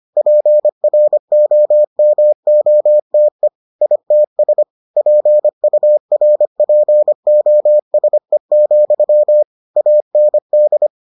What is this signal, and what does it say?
Morse code, heard as a keyed tone, about 25 wpm, 600 Hz. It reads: PROMOTE ITS PURPOSE, AND